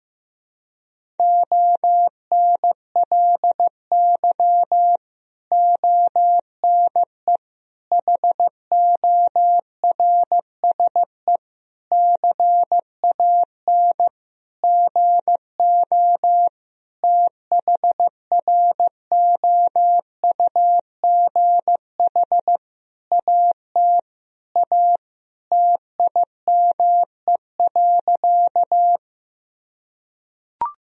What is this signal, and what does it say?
Morse code, heard as a keyed tone, about 15 wpm, 700 Hz. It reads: ONLY ONE HORSE CAN GO THROUGH AT A TIME.
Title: Sentences from Top 1000 Words - Review Part 1 - 15wpm